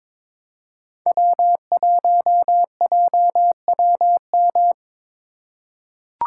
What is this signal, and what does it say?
Morse code, heard as a keyed tone, about 22 wpm, 700 Hz. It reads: W1JWM
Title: USA Call Signs - 5 Characters - 4x Character Spacing - 22wpm